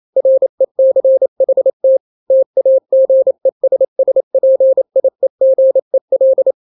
Morse code, heard as a keyed tone, about 28 wpm, 530 Hz. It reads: RECHT TAGESSPIEGEL
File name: daily / 27wpm